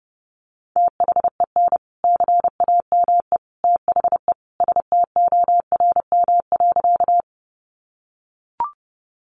Morse code, heard as a keyed tone, about 30 wpm, 700 Hz. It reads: THEN CAME THE STORM.
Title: Sentences from Top 1500 Words 30wpm